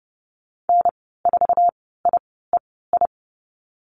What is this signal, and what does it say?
Morse code, heard as a keyed tone, about 30 wpm, spaced as for 17 wpm, 700 Hz. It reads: N4IEI
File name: Lesson 14g - Number 2 - Call Signs 30wpm